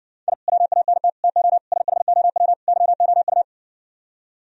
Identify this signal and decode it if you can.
Morse code, heard as a keyed tone, about 60 wpm, 700 Hz. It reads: I WANT TO SHOW YOU